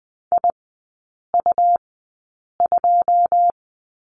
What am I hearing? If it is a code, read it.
Morse code, heard as a keyed tone, about 20 wpm, spaced as for 8 wpm, 700 Hz. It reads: IU2
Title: 121 Lesson 20e - Number 3 - Three Characters 20wpm